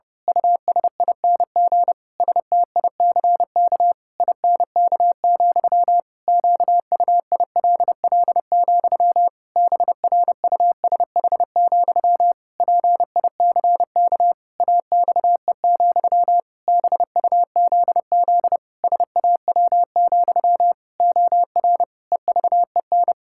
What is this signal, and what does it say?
Morse code, heard as a keyed tone, about 30 wpm, 700 Hz. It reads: USING STICK INK, QUILL, BRUSH, PICK AXE, BUZZ SAW, OR EVEN
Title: Pangram Challenge v2 - 30wpm